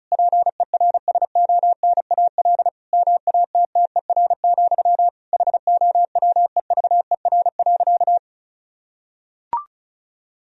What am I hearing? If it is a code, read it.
Morse code, heard as a keyed tone, about 36 wpm, 700 Hz. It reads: PERSONAL MATTER, HOWEVER.
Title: Sentences from Top 1000 Words - Review Part 4 - 35wpm